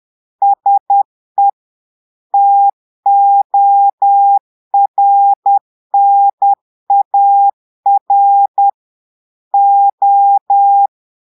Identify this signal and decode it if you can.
Morse code, heard as a keyed tone, about 10 wpm, 800 Hz. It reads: SE TORNAR O
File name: portugais texte en clair 2 vitesse 10 mots minutes